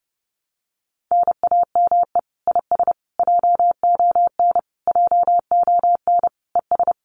Morse code, heard as a keyed tone, about 32 wpm, 700 Hz. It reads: NAME IS JON JONES
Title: Common QSO Phrases Encore 30wpm